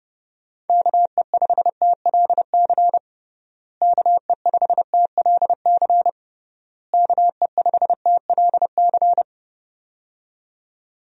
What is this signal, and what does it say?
Morse code, heard as a keyed tone, about 29 wpm, 700 Hz. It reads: KE5TLC KE5TLC KE5TLC
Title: USA Call Signs - 6 Characters - Repeated 3x - 30wpm